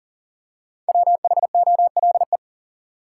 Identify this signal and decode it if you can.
Morse code, heard as a keyed tone, about 40 wpm, 700 Hz. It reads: WHOLE
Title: Five Letter Words - 40wpm